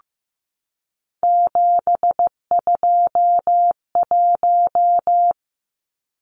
Morse code, heard as a keyed tone, about 15 wpm, 700 Hz. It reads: 721
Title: Numbers - 3-Digit 15wpm